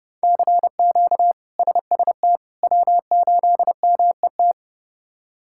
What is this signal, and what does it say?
Morse code, heard as a keyed tone, about 32 wpm, 700 Hz. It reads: CQ SST W8MET
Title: K1USN SST QSO Practice - 30wpm